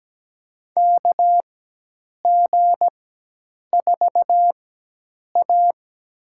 Morse code, heard as a keyed tone, about 17 wpm, spaced as for 8 wpm, 700 Hz. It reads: KG4A
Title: USA Call Signs - 4 Characters - 4x Character Spacing - 17wpm